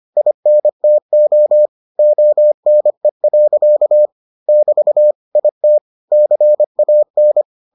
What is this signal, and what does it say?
Morse code, heard as a keyed tone, about 25 wpm, 600 Hz. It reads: INTO ONE. <BT> IT CAN